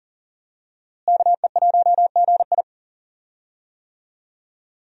Morse code, heard as a keyed tone, about 40 wpm, 700 Hz. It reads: KE1GI
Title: USA Call Signs - 5 Characters - Repeated 2x - 40wpm